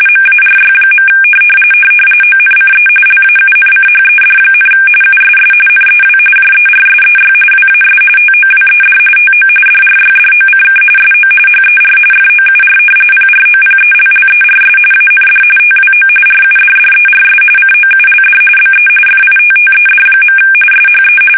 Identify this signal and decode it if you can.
RTTY, about 45.45 baud. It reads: MARY. ONCE LAUNCHED, USS NEW JERSEY WAS TOWED TO DRYDOCK SO HER FOUR PROPELLERS AND TWO RUDDERS COULD BE FITTED. SHE
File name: fsk850 / battleshipnj